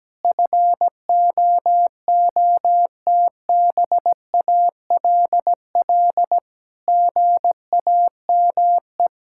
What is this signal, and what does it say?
Morse code, heard as a keyed tone, about 17 wpm, 700 Hz. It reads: FOOTBALL GAME
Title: Sentences from Top 1600 Words 17wpm